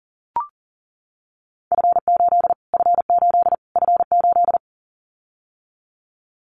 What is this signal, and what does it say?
Morse code, heard as a keyed tone, about 40 wpm, 700 Hz. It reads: F8 F8 F8
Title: Callsign Prefixes for Top 25 DXCC - Repeated 3x - 40wpm